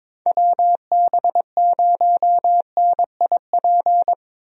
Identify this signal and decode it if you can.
Morse code, heard as a keyed tone, about 22 wpm, 700 Hz. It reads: WB0NIP
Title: USA Call Signs - 6 Characters - Repeated 2x - 22wpm